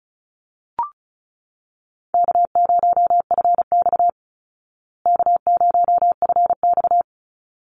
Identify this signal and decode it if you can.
Morse code, heard as a keyed tone, about 34 wpm, 700 Hz. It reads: K0FX K0FX
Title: Easy Call Signs Repeated 2x - 35wpm